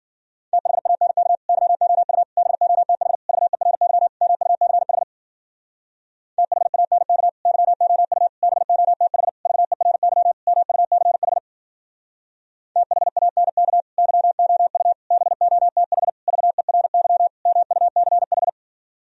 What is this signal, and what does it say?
Morse code, heard as a keyed tone, about 50 wpm, 700 Hz. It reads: THANK YOU BOTH VERY MUCH THANK YOU BOTH VERY MUCH THANK YOU BOTH VERY MUCH
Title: Speed-Racing - Sets of 5 Words 40wpm